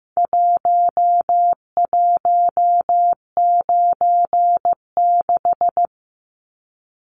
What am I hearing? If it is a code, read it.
Morse code, heard as a keyed tone, about 15 wpm, 700 Hz. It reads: 1196
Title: Numbers - Contest Style - 4-Digit 15wpm